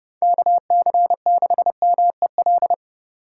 Morse code, heard as a keyed tone, about 30 wpm, 700 Hz. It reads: KC6MEL
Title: USA Call Signs - 4x Character Spacing - 30wpm